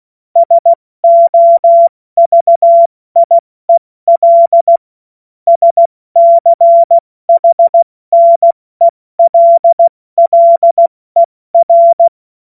Morse code, heard as a keyed tone, about 16 wpm, spaced as for 14 wpm, 680 Hz. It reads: SOVIEL SCHNELLER
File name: Morsekurs / Witze / Witz 30. August 2022 70 F80